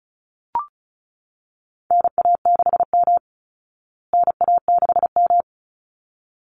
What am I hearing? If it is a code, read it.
Morse code, heard as a keyed tone, about 34 wpm, 700 Hz. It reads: NA6M NA6M
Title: Call Signs Repeated 3x - 35wpm